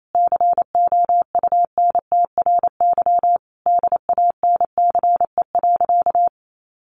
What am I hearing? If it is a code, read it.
Morse code, heard as a keyed tone, about 28 wpm, 700 Hz. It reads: COUNTRY DANCE.
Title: Sentences from Top 1300 Words 28wpm